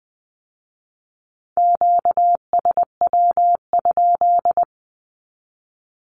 Morse code, heard as a keyed tone, about 19 wpm, 700 Hz. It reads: QSW?
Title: Q-Codes with Spoken Meaning - 20wpm